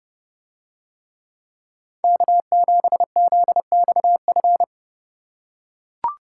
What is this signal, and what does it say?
Morse code, heard as a keyed tone, about 29 wpm, 700 Hz. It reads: K7ZXF